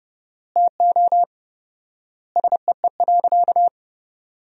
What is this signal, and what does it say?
Morse code, heard as a keyed tone, about 30 wpm, 700 Hz. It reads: TO SEE.
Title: 4x Word-Spacing - Sentences from Top 500 Words - 30wpm